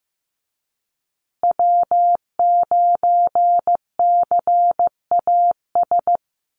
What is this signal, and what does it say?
Morse code, heard as a keyed tone, about 15 wpm, 700 Hz. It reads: W9CAS
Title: USA Call Signs Repeated 2x 15wpm